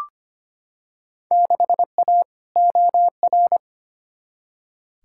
Morse code, heard as a keyed tone, about 25 wpm, 700 Hz. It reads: 6A OR
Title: Field Day Exchanges - 25wpm